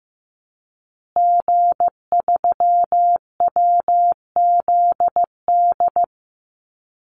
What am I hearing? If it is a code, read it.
Morse code, heard as a keyed tone, about 15 wpm, 700 Hz. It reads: G3WZD
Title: Call Signs Repeated 2x - 15wpm